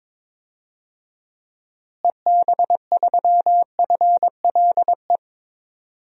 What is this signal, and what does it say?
Morse code, heard as a keyed tone, about 22 wpm, 700 Hz. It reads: EB3FLE
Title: DXCC-100 Call Signs - 6 Characters - Repeated 2x - 22wpm